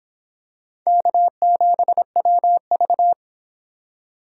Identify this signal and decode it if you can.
Morse code, heard as a keyed tone, about 25 wpm, 700 Hz. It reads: K7WV